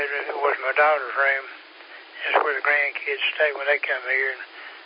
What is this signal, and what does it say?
Emergency Alert System (SAME) tones, heard as harsh buzzing bursts, with speech. It is heard from 1.1 to 4.3 s.